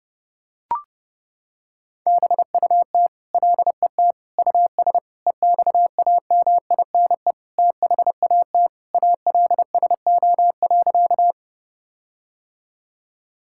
Morse code, heard as a keyed tone, about 32 wpm, 700 Hz. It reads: BUT LET US EXAMINE THAT ALSO.
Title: Sentences from Top 1600 Words 30wpm